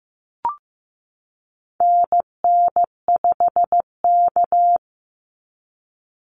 Morse code, heard as a keyed tone, about 15 wpm, 700 Hz. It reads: NN5K
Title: NA SOTA Activator Callsigns 15wpm